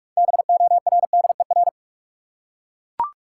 Morse code, heard as a keyed tone, about 41 wpm, 700 Hz. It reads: BORDER
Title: Six Letter Words - 45wpm